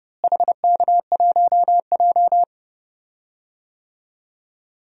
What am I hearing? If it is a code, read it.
Morse code, heard as a keyed tone, about 30 wpm, 700 Hz. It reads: HK1J
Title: DXCC-100 Call Signs - 4 Characters - Repeated 2x - 30wpm